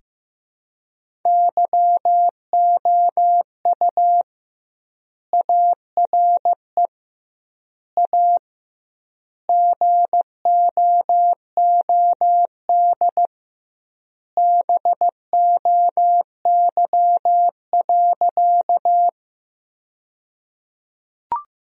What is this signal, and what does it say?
Morse code, heard as a keyed tone, about 16 wpm, 700 Hz. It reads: YOU ARE A GOOD BOY.
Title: 2x Word-Spacing - Sentences from Top 200 Words - 15wpm